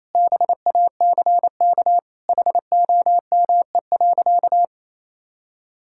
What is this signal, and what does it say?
Morse code, heard as a keyed tone, about 28 wpm, 700 Hz. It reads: BACK HOME.
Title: Sentences from Top 1000 Words - Review Part 6 - 28wpm